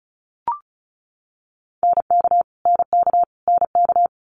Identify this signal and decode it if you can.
Morse code, heard as a keyed tone, about 36 wpm, 700 Hz. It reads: NK NK NK
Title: Multi-letter Phonemes Repeated 3x 35wpm